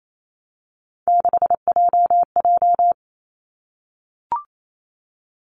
Morse code, heard as a keyed tone, about 28 wpm, 700 Hz. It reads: 6JJ
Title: Three Challenging Letter-Number Combinations 28wpm